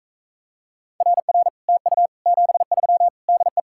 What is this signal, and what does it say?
Morse code, heard as a keyed tone, about 44 wpm, 700 Hz. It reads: RR TU 73 DE